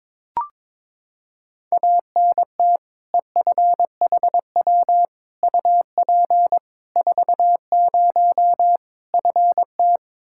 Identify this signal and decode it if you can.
Morse code, heard as a keyed tone, about 22 wpm, 700 Hz. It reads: ANT EFHW UP 40 FT
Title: Practice QSO - Exchange 2 - Info - 22wpm